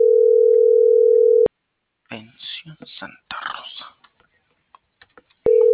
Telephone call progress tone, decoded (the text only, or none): ringback tone